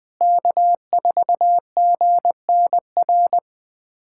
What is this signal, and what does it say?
Morse code, heard as a keyed tone, about 21 wpm, 700 Hz. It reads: K4GNR
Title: USA Callsign DE Callsign - 5 Characters - 20wpm